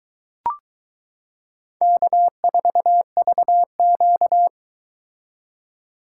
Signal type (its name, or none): Morse code